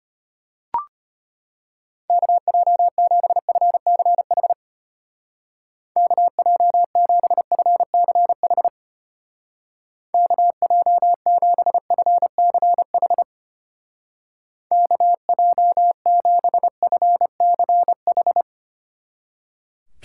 Morse code, heard as a keyed tone, about 31 wpm, 700 Hz. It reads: KJ7FCH KJ7FCH KJ7FCH KJ7FCH